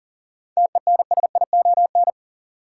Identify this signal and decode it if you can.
Morse code, heard as a keyed tone, about 40 wpm, 700 Hz. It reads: TENSION
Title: Sentences from Top 1600 Words - New Word Drill - 40wpm